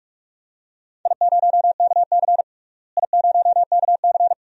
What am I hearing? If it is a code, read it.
Morse code, heard as a keyed tone, about 46 wpm, 700 Hz. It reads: I0KC I0KC
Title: DXCC-100 Call Signs - 4 Characters - Repeated 2x - 45wpm